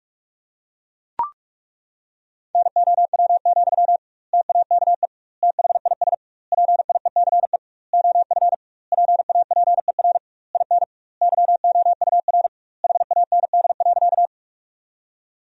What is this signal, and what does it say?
Morse code, heard as a keyed tone, about 41 wpm, 700 Hz. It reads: NOW, TAKE THIS PIECE OF PAPER IN YOUR HAND.
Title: 2x Word-Spacing - Sentences from Top 400 Words - 45wpm